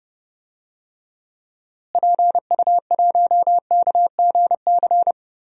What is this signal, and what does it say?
Morse code, heard as a keyed tone, about 30 wpm, 700 Hz. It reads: PU1KGC
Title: DXCC-100 Call Signs - 6 Characters - Repeated 3x - 30wpm